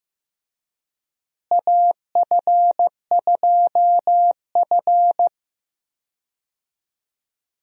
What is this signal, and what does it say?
Morse code, heard as a keyed tone, about 15 wpm, 700 Hz. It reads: AF2F
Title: USA Call Signs - 4 Characters - Repeated 3x - 15wpm